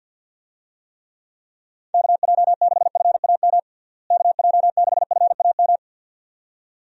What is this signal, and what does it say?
Morse code, heard as a keyed tone, about 51 wpm, 700 Hz. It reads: KJ6FAM KJ6FAM
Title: USA Call Signs - 6 Characters - Repeated 2x - 50wpm